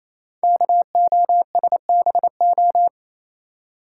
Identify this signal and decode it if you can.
Morse code, heard as a keyed tone, about 27 wpm, 700 Hz. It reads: KOSBO
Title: Code Groups - 5 Letters - 28wpm